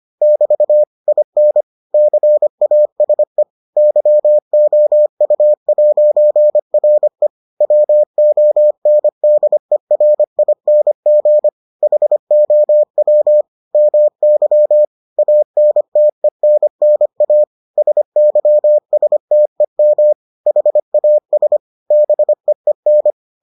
Morse code, heard as a keyed tone, about 25 wpm, 600 Hz. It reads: <BT> IN CASE YOU'RE WONDERING HOW MY ANTENNA SYSTEM HAS BEEN